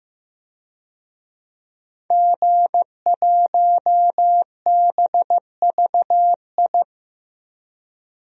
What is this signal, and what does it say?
Morse code, heard as a keyed tone, about 15 wpm, 700 Hz. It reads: G1BVI
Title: DXCC-100 Call Signs - 5 Characters - 15wpm